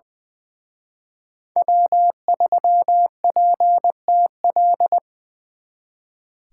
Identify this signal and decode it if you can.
Morse code, heard as a keyed tone, about 24 wpm, 700 Hz. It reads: W3PTL
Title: Speed Racing - USA 1x3 Call Signs 20wpm